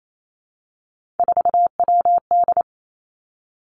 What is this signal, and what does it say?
Morse code, heard as a keyed tone, about 28 wpm, 700 Hz. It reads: 4WD
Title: Three Challenging Letter-Number Combinations 28wpm